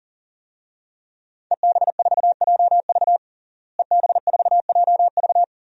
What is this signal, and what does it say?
Morse code, heard as a keyed tone, about 40 wpm, 700 Hz. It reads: EB4JV EB4JV
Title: DXCC-100 Call Signs - 5 Characters - Repeated 3x - 40wpm